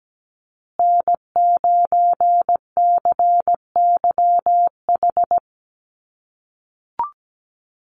Morse code, heard as a keyed tone, about 17 wpm, 700 Hz. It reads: N9CYH
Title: USA Call Signs Repeated 3x 17wpm